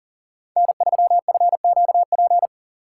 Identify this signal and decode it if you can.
Morse code, heard as a keyed tone, about 38 wpm, 700 Hz. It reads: N3FQP
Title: USA Call Signs - 5 Characters - 2x Character Spacing - 40wpm